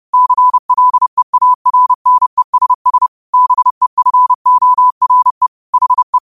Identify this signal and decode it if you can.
Morse code, heard as a keyed tone, about 30 wpm, 1000 Hz. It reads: CLEARNESS BEFORE HE
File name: qotdfiles / 30wpm